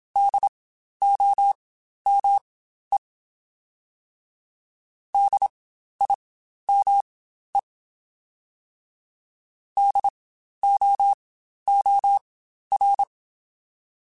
Morse code, heard as a keyed tone, about 27 wpm, spaced as for 12 wpm, 800 Hz. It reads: DOME DIME DOOR
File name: Spaced Alphas / Spaced Alphas mp3s / Ds sp @25